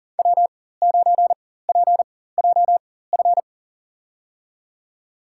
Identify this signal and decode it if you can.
Morse code, heard as a keyed tone, about 40 wpm, spaced as for 19 wpm, 700 Hz. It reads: W9PJF
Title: USA Call Signs - 5 Characters - 4x Character Spacing - 40wpm